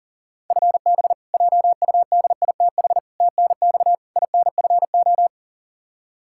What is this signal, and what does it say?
Morse code, heard as a keyed tone, about 40 wpm, 700 Hz. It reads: FB JUDITH TNX INFO
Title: K1USN SST QSO Practice 40wpm